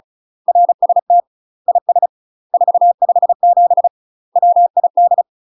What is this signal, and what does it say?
Morse code, heard as a keyed tone, about 36 wpm, 700 Hz. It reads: RST IS 457 WID
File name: UK MASTER QSOs TRAINING/35WPM